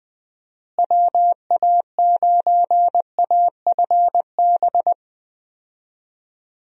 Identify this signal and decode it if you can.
Morse code, heard as a keyed tone, about 20 wpm, 700 Hz. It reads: WA9AFB
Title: USA Call Signs - 6 Characters - Repeated 3x - 20wpm